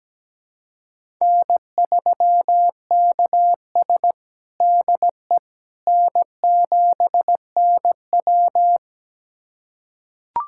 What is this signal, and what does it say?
Morse code, heard as a keyed tone, about 17 wpm, 700 Hz. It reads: N3KS DE N7NW
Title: USA Callsign DE Callsign - 4 Characters - 17wpm